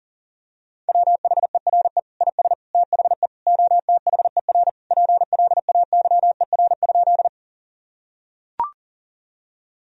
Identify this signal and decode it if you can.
Morse code, heard as a keyed tone, about 37 wpm, 700 Hz. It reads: WHERE IS THE OTHER PLAYER?
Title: Sentences from Top 1000 Words - Review Part 4 - 40wpm